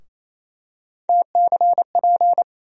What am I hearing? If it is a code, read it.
Morse code, heard as a keyed tone, about 28 wpm, 700 Hz. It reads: TCP
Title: USA Call Sign Postfixes - 3 Letters 28wpm